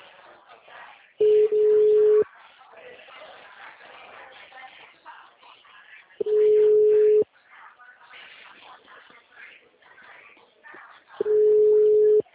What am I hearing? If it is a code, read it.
Ringback tone (Europe, 425 Hz), with speech.